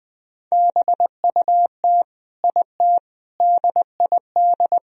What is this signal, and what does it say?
Morse code, heard as a keyed tone, about 20 wpm, 700 Hz. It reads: BUT IT DID
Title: Sentences from Top 1000 Words - Review Part 2 - 20wpm